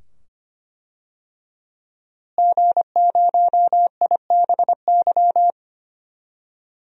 Morse code, heard as a keyed tone, about 25 wpm, 700 Hz. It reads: G0IBY
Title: DXCC-100 Call Signs - 5 Characters - Repeated 2x - 25wpm